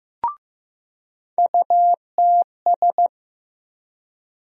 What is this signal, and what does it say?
Morse code, heard as a keyed tone, about 15 wpm, 700 Hz. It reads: UTS
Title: USA Call Sign Postfixes - 3 Letters 15wpm